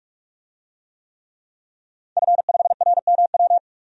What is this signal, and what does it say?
Morse code, heard as a keyed tone, about 46 wpm, 700 Hz. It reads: F5RMW
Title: DXCC-100 Call Signs - 5 Characters - 45wpm